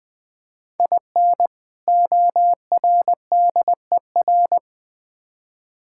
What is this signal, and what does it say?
Morse code, heard as a keyed tone, about 21 wpm, 700 Hz. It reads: IN ORDER
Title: Build A Phrase - 2-Word Phrases - 20wpm